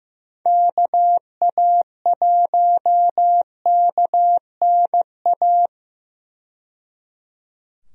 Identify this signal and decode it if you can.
Morse code, heard as a keyed tone, about 16 wpm, 700 Hz. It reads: KA1KNA
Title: USA Call Signs - 6 Characters - 15wpm